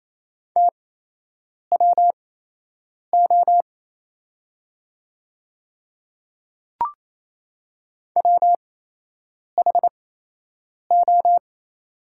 Morse code, heard as a keyed tone, about 27 wpm, spaced as for 8 wpm, 690 Hz. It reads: TWO WHO
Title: 8x Character Spacing - Top 200 Words - 28wpm